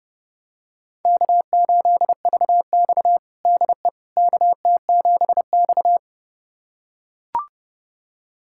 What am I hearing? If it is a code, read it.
Morse code, heard as a keyed tone, about 31 wpm, 700 Hz. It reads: K8VX DE KT7X
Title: USA Callsign DE Callsign - 4 Characters - 30wpm